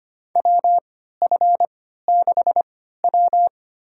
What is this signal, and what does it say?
Morse code, heard as a keyed tone, about 25 wpm, spaced as for 14 wpm, 700 Hz. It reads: WF6W
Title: USA Call Signs - 4 Characters - 3x Character Spacing - 25wpm